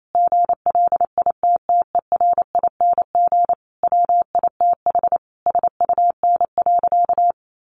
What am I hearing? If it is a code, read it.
Morse code, heard as a keyed tone, about 28 wpm, 700 Hz. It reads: GLITTERING WITH SUN.
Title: A Sound Of Thunder 28wpm